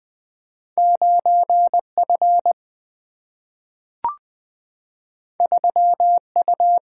Morse code, heard as a keyed tone, about 20 wpm, 700 Hz. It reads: 9 F 3 U
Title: Single-Letter Single-Number 20wpm